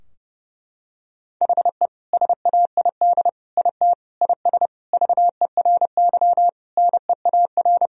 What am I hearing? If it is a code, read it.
Morse code, heard as a keyed tone, about 30 wpm, 700 Hz. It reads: HE SAID IT IS VERY NEAR
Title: Sentences from Top 1000 Words - Review Part 2 - 30wpm